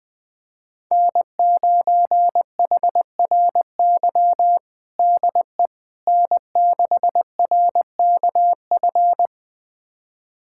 Morse code, heard as a keyed tone, about 20 wpm, 700 Hz. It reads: N9HRY DE N6RKF
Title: USA Callsign DE Callsign - 5 Characters - 20wpm